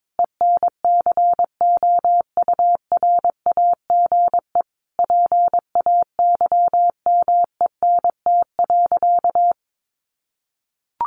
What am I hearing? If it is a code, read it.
Morse code, heard as a keyed tone, about 22 wpm, 700 Hz. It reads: ENCOURAGE PAYMENT.
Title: Sentences from Top 1400 Words 22wpm